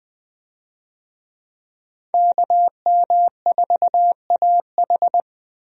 Morse code, heard as a keyed tone, about 20 wpm, 700 Hz. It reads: KM4AH
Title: USA Call Signs - 5 Characters - Repeated 2x - 20wpm